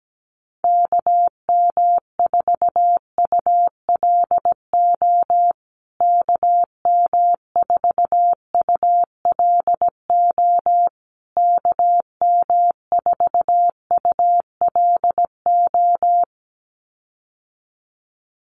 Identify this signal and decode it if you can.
Morse code, heard as a keyed tone, about 17 wpm, 700 Hz. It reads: KM4ULO KM4ULO KM4ULO
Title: USA Call Signs Repeated 3x 17wpm